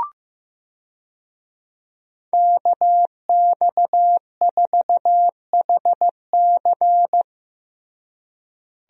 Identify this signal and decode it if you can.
Morse code, heard as a keyed tone, about 15 wpm, 700 Hz. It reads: KX4HC
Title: USA Call Signs - 5 Characters - Repeated 2x - 15wpm